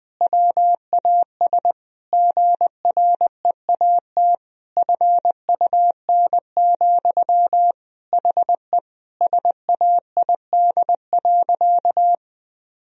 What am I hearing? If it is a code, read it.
Morse code, heard as a keyed tone, about 20 wpm, 700 Hz. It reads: WAS GREAT FUN, HE SAID.
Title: Sentences from Top 1000 Words - Review Part 4 - 20wpm